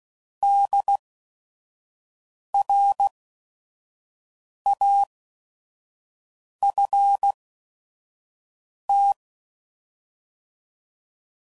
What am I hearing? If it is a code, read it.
Morse code, heard as a keyed tone, about 16 wpm, spaced as for 5 wpm, 800 Hz. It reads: DRAFT
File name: Prefixes / By First Letter / Ds